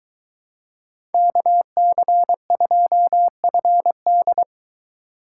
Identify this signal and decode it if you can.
Morse code, heard as a keyed tone, about 23 wpm, 700 Hz. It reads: KC2FD